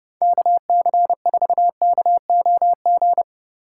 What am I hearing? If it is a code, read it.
Morse code, heard as a keyed tone, about 37 wpm, 700 Hz. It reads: KC4KOG
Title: Speed Racing - USA 2x3 Call Signs 30wpm